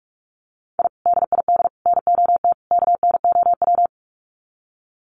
Morse code, heard as a keyed tone, about 44 wpm, 700 Hz. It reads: I DID NOT KNOW
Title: Sets of 4 Words 45wpm